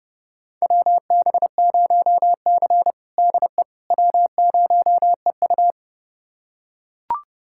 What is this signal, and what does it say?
Morse code, heard as a keyed tone, about 32 wpm, 700 Hz. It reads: WB0C DE W0EU
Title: USA Callsign DE Callsign - 4 Characters - 30wpm